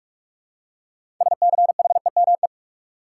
Morse code, heard as a keyed tone, about 46 wpm, 700 Hz. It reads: SCHEME
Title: Six Letter Words - 45wpm